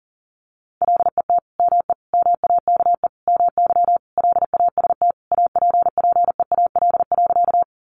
Morse code, heard as a keyed tone, about 40 wpm, 700 Hz. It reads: LET ME MAKE MY LAST APPEAL.
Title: Sentences from Top 1300 Words 40wpm